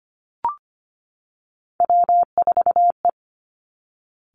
Morse code, heard as a keyed tone, about 25 wpm, 700 Hz. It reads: W4E
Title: Three Challenging Letter-Number Combinations 25wpm